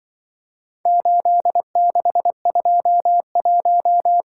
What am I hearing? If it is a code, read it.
Morse code, heard as a keyed tone, about 24 wpm, 700 Hz. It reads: 8621
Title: Speed-Racing - Four Letter-Number Combinations - 20wpm